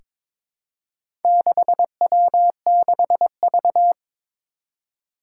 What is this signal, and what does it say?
Morse code, heard as a keyed tone, about 22 wpm, 700 Hz. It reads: 6W6V
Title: Four Challenging Letter-Number Combinations 22wpm